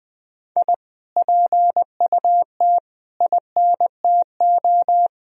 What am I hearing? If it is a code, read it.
Morse code, heard as a keyed tone, about 20 wpm, 700 Hz. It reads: I PUT INTO
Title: Sentences from Top 1000 Words - Review Part 2 - 20wpm